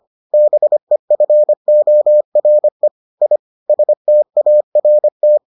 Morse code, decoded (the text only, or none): BEFORE I START